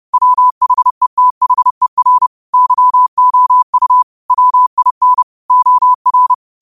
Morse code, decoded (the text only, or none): WHETHER YOU WIN OR